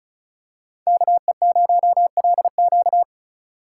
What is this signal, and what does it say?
Morse code, heard as a keyed tone, about 36 wpm, 700 Hz. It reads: KE0LQ
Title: USA Call Signs - 5 Characters - 3x Character Spacing - 35wpm